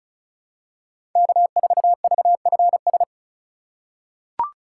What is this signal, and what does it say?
Morse code, heard as a keyed tone, about 36 wpm, 700 Hz. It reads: K4VFS
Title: USA Call Signs - 5 Characters - 3x Character Spacing - 35wpm